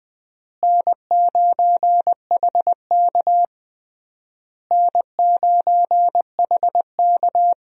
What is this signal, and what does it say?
Morse code, heard as a keyed tone, about 20 wpm, 700 Hz. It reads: N9HK N9HK
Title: USA Call Signs - 4 Characters - Repeated 3x - 20wpm